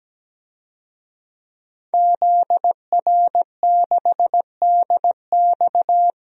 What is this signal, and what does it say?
Morse code, heard as a keyed tone, about 17 wpm, 700 Hz. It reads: ZR6DX
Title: DXCC-100 Call Signs - 5 Characters - 17wpm